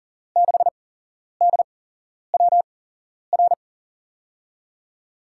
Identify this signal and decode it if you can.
Morse code, heard as a keyed tone, about 40 wpm, spaced as for 11 wpm, 700 Hz. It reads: 6DWR
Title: USA Call Signs - 5 Characters - 8x Character Spacing - 40wpm